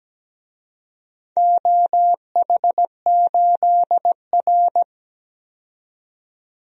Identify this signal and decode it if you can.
Morse code, heard as a keyed tone, about 17 wpm, 700 Hz. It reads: OH8R
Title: DXCC-100 Call Signs - 4 Characters - 17wpm